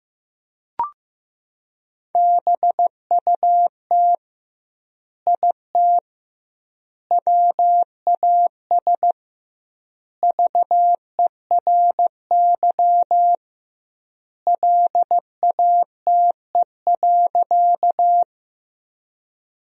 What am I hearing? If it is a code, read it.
Morse code, heard as a keyed tone, about 15 wpm, 700 Hz. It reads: BUT IT WAS VERY LATE.
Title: 2x Word-Spacing - Sentences from Top 300 Words - 15wpm